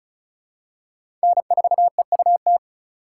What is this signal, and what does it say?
Morse code, heard as a keyed tone, about 36 wpm, 700 Hz. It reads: N4EUT
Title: USA Call Signs - 5 Characters - 6x Character Spacing - 35wpm